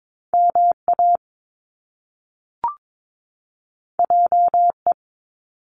Morse code, heard as a keyed tone, about 22 wpm, 700 Hz. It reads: MA JE
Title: Top Two Letter Combinations in Words 22wpm